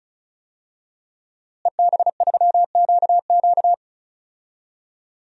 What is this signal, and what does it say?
Morse code, heard as a keyed tone, about 36 wpm, 700 Hz. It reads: EB3QQ
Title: DXCC-100 Call Signs - 5 Characters - Repeated 2x - 35wpm